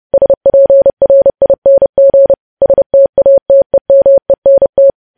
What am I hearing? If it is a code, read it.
Morse code, heard as a keyed tone, about 30 wpm, 550 Hz. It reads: SPRING STATEMENT
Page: news headlines in morse code at 30 wpm